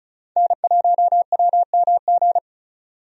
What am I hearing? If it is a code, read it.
Morse code, heard as a keyed tone, about 36 wpm, 700 Hz. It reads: N1WMG
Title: USA Call Signs - 5 Characters - 35wpm